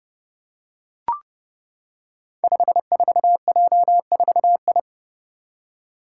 Morse code, heard as a keyed tone, about 30 wpm, 700 Hz. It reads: 54J4I
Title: Five Challenging Letter-Number Combinations 30wpm